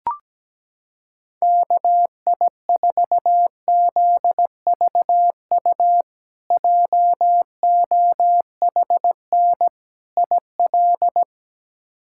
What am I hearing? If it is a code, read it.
Morse code, heard as a keyed tone, about 17 wpm, 700 Hz. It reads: KI4ZVU JOHN IL
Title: K1USN SST QSO Practice - 17wpm